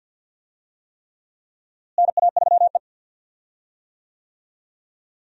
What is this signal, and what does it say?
Morse code, heard as a keyed tone, about 52 wpm, 700 Hz. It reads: NA3E
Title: USA Call Signs - 4 Characters - Repeated 3x - 50wpm